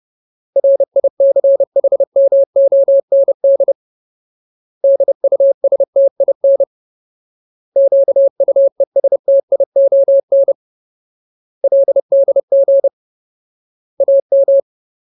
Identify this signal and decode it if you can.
Morse code, heard as a keyed tone, about 30 wpm, 550 Hz. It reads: RICHMOND DUSTIN QUESTION LDG AM
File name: Ultimate Word List / 30WPM Ultimate